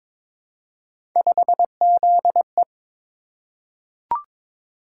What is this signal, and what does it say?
Morse code, heard as a keyed tone, about 22 wpm, 700 Hz. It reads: 5ZE
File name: practice / 22wpm / Three Letter-Number Combinations - 22wpm